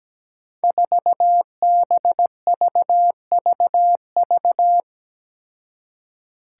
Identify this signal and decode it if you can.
Morse code, heard as a keyed tone, about 17 wpm, 700 Hz. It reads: 4BVVV
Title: Five Challenging Letter-Number Combinations 17wpm